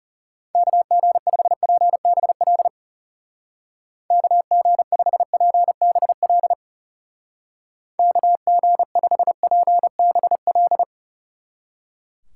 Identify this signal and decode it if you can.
Morse code, heard as a keyed tone, about 36 wpm, 700 Hz. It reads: KG5PBL KG5PBL KG5PBL
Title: Speed Racing - USA 2x3 Call Signs 30wpm